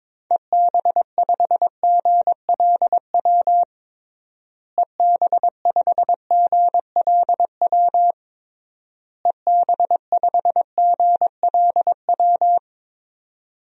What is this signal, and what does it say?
Morse code, heard as a keyed tone, about 22 wpm, 700 Hz. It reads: EB5GLW EB5GLW EB5GLW
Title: DXCC-100 Call Signs - 6 Characters - Repeated 3x - 22wpm